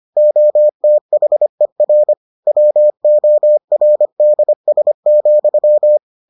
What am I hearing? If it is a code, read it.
Morse code, heard as a keyed tone, about 25 wpm, 600 Hz. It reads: OTHER WORDS,